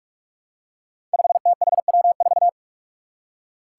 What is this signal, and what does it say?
Morse code, heard as a keyed tone, about 46 wpm, 700 Hz. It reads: 5THW4
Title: Five Challenging Letter-Number Combinations 45wpm